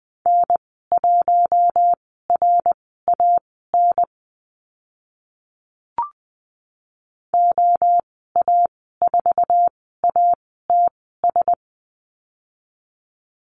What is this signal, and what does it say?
Morse code, heard as a keyed tone, about 20 wpm, spaced as for 15 wpm, 700 Hz. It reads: N1RAN OA4ATS
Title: 069 Lesson 12g - Letter d - Call Signs 20wpm